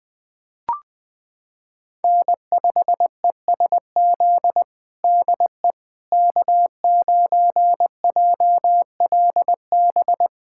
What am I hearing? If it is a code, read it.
Morse code, heard as a keyed tone, about 21 wpm, 700 Hz. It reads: N5ESZ DE K9JLB
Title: USA Callsign DE Callsign - 5 Characters - 20wpm